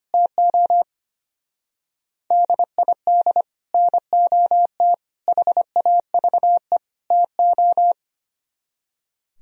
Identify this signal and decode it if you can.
Morse code, heard as a keyed tone, about 30 wpm, 700 Hz. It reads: TO DID NOT HAVE TO
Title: Speed-Racing - Sets of 4 Words 25wpm